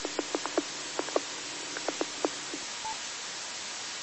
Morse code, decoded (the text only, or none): none